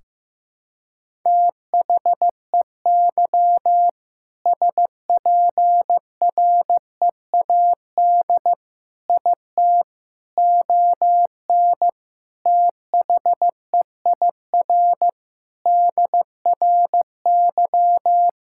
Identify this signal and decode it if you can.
Morse code, heard as a keyed tone, about 15 wpm, 700 Hz. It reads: THEY SPREAD IT ON THEIR DRY
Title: Sentences from Top 1600 Words 15wpm